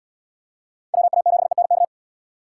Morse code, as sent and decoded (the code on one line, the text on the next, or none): .-- .- --... -. -.-
WA7ENK